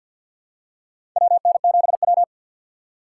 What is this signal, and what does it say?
Morse code, heard as a keyed tone, about 50 wpm, 700 Hz. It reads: WN7W
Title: USA Call Signs - 4x Character Spacing - 50wpm